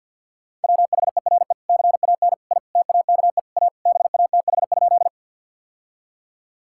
Morse code, heard as a keyed tone, about 50 wpm, 700 Hz. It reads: WHERE CAN I TAKE A BATH?
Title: Sentences from Top 1600 Words 50wpm